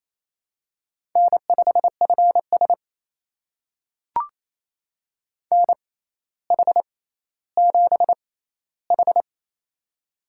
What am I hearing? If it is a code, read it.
Morse code, heard as a keyed tone, about 27 wpm, spaced as for 14 wpm, 700 Hz. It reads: N5FS NH7H
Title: USA Call Signs - 6x Character Spacing - 28wpm